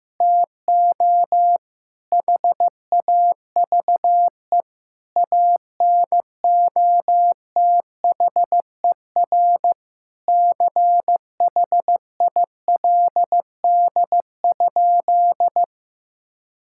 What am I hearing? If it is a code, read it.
Morse code, heard as a keyed tone, about 15 wpm, 700 Hz. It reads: TO HAVE ANOTHER CHILD?
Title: Sentences from Top 1600 Words 15wpm